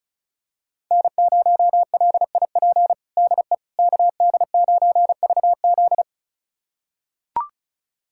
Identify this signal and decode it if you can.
Morse code, heard as a keyed tone, about 36 wpm, 700 Hz. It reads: N0LIP DE KD9VZ
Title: USA Callsign DE Callsign - 5 Characters - 35wpm